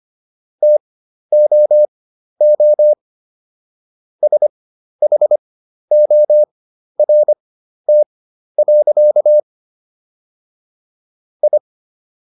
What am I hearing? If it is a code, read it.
Morse code, heard as a keyed tone, about 25 wpm, spaced as for 12 wpm, 600 Hz. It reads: TOO SHORT. I